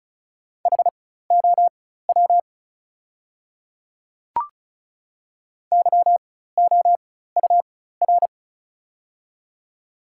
Morse code, heard as a keyed tone, about 36 wpm, spaced as for 16 wpm, 690 Hz. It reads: HOW YOUR
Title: 4x Character Spacing - Top 100 Words - 35wpm